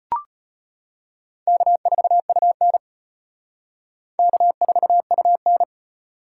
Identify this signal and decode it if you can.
Morse code, heard as a keyed tone, about 31 wpm, 700 Hz. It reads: K4UN K4UN
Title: Speed Racing - USA 1x2 Call Signs 25wpm